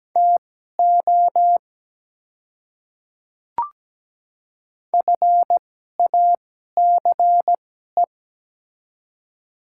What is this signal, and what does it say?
Morse code, heard as a keyed tone, about 17 wpm, spaced as for 12 wpm, 700 Hz. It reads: TO FACE
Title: 2x Character Spacing - Top 300 Words - 17wpm